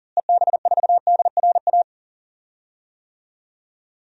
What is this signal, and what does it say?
Morse code, heard as a keyed tone, about 39 wpm, 700 Hz. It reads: EB4DRA
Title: DXCC-100 Call Signs - 6 Characters - Repeated 3x - 40wpm